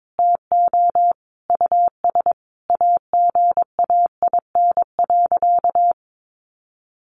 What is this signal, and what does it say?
Morse code, heard as a keyed tone, about 22 wpm, 700 Hz. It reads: TO US AGAIN.
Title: Sentences from Top 1100 Words 22wpm